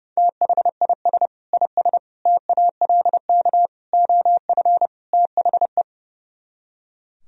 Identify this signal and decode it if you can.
Morse code, heard as a keyed tone, about 35 wpm, 700 Hz. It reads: THIS IS TALK OF THE
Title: Speed-Racing - Sets of 5 Words 30wpm